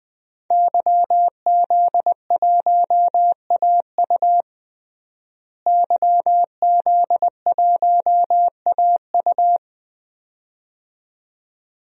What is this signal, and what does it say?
Morse code, heard as a keyed tone, about 20 wpm, 700 Hz. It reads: YZ1AU YZ1AU
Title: DXCC-100 Call Signs - 5 Characters - Repeated 2x - 20wpm